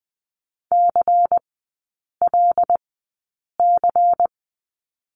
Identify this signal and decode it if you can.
Morse code, heard as a keyed tone, about 20 wpm, spaced as for 8 wpm, 700 Hz. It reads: CLC
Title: 093 Lesson 16e - Letter c - Three Characters 20wpm